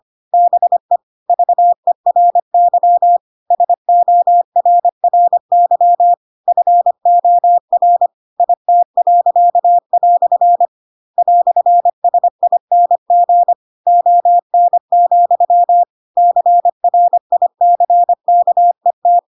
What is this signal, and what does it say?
Morse code, heard as a keyed tone, about 25 wpm, 700 Hz. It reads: BE VERY SORRY FOR IT." "SING ON, CRICKET
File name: Pinocchio / Pinocchio ch4 25 25 ik1hge